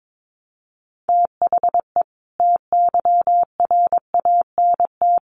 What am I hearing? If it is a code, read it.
Morse code, heard as a keyed tone, about 22 wpm, 700 Hz. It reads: THE TYRANT
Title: A Sound Of Thunder 22wpm